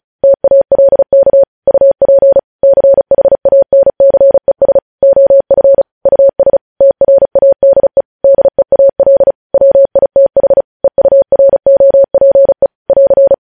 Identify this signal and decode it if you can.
Morse code, heard as a keyed tone, about 36 wpm, 550 Hz. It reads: TALK UP CHANCES OF US TRADE DEAL WITH EUROPE <AR>
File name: podcast / News headlines-1x-35x35